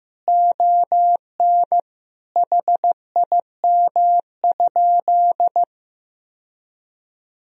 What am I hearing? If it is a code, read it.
Morse code, heard as a keyed tone, about 15 wpm, 700 Hz. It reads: ON HIM?
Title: Sentences from Top 1000 Words - Review Part 2 - 15wpm